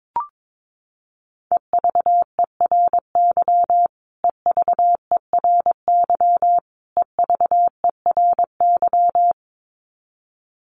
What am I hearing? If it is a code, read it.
Morse code, heard as a keyed tone, about 22 wpm, 700 Hz. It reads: EVERY EVERY EVERY
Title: Top 500 Words Repeated 22wpm